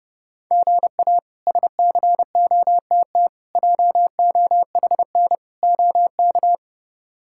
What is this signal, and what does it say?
Morse code, heard as a keyed tone, about 30 wpm, 700 Hz. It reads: GA SCOTT JOHN OK
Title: K1USN SST QSO Practice - 30wpm